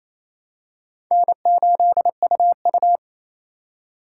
Morse code, heard as a keyed tone, about 28 wpm, 700 Hz. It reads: N8UU
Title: USA Call Signs - 4 Characters - 8x Character Spacing - 28wpm